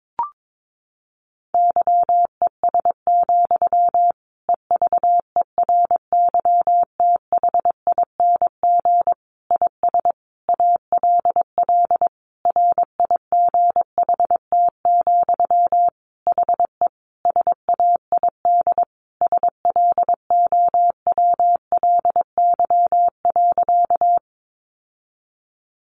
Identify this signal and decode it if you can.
Morse code, heard as a keyed tone, about 22 wpm, 700 Hz. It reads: YES, EVERYTHING IS ALL RIGHT, HE SAID SLOWLY.
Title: Sentences from Top 1100 Words 22wpm